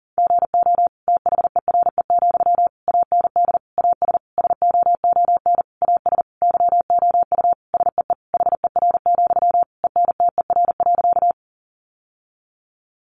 Morse code, heard as a keyed tone, about 40 wpm, 700 Hz. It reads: GO THERE, AND AS SOON AS YOU SEE HER, ENTER.
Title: Sentences from Top 1300 Words 40wpm